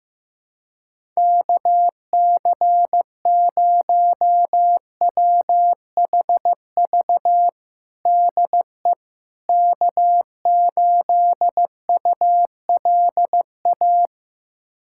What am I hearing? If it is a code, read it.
Morse code, heard as a keyed tone, about 15 wpm, 700 Hz. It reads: KC0WHV DE K8ULA
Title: USA Callsign DE Callsign 15wpm